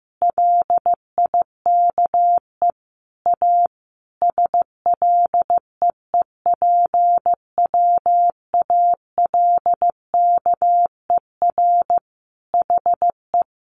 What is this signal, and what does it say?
Morse code, heard as a keyed tone, about 15 wpm, 700 Hz. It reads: LIKE A SLEEPWALKER HE
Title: A Sound Of Thunder 15wpm